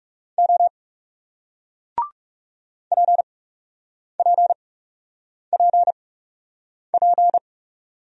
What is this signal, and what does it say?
Morse code, heard as a keyed tone, about 40 wpm, spaced as for 8 wpm, 700 Hz. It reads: O PPPP